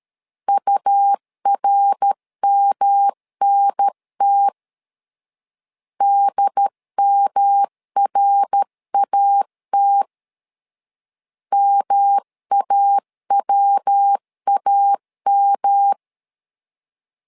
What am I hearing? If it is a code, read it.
Morse code, heard as a keyed tone, about 13 wpm, 800 Hz. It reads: URMNT DMRAT MAWAM
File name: LEZIONI / 3 MWRO / 3 MWR 12wpm 1711